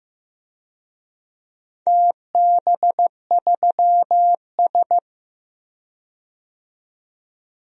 Morse code, heard as a keyed tone, about 15 wpm, 700 Hz. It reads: TB3S
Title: DXCC-100 Call Signs - 4 Characters - 15wpm